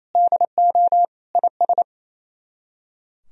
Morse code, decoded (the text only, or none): DO IS